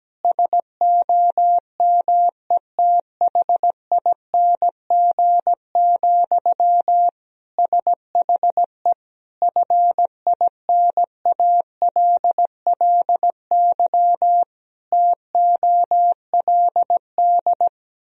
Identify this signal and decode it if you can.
Morse code, heard as a keyed tone, about 17 wpm, 700 Hz. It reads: SOMETHING, SHE FINALLY TOLD
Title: Sentences from Top 1000 Words - Review Part 3 - 17wpm